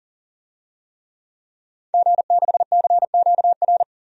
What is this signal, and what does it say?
Morse code, heard as a keyed tone, about 40 wpm, 700 Hz. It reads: G6CQR
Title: DXCC-100 Call Signs - 5 Characters - 40wpm